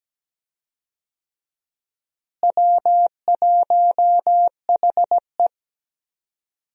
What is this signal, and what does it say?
Morse code, heard as a keyed tone, about 17 wpm, 700 Hz. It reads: W1HE
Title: USA Call Signs - 4 Characters - Repeated 3x - 17wpm